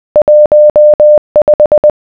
Morse code, heard as a keyed tone, about 20 wpm, 600 Hz. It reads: 15